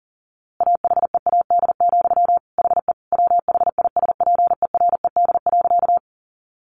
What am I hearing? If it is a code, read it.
Morse code, heard as a keyed tone, about 40 wpm, 700 Hz. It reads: AHEAD, HE WHISPERED.
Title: A Sound Of Thunder 40wpm